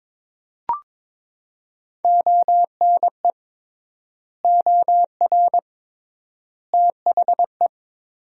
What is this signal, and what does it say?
Morse code, heard as a keyed tone, about 22 wpm, 700 Hz. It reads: ONE OR THE
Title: 3x Word-Spacing - Sentences from Top 400 Words - 22wpm